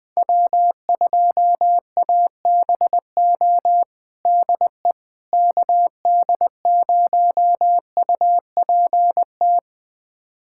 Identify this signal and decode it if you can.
Morse code, heard as a keyed tone, about 20 wpm, 700 Hz. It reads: W2ABO DE KD0UPT
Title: USA Callsign DE Callsign 20wpm